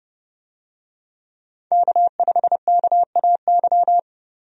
Morse code, heard as a keyed tone, about 29 wpm, 700 Hz. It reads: K5KAY
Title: USA Call Signs - 5 Characters - 30wpm